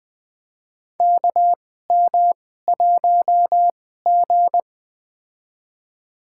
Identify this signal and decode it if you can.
Morse code, heard as a keyed tone, about 20 wpm, spaced as for 14 wpm, 700 Hz. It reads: KM1G
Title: USA Call Signs - 4 Characters - 2x Character Spacing - 20wpm